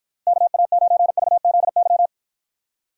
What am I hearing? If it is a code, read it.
Morse code, heard as a keyed tone, about 53 wpm, 700 Hz. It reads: KA9VZY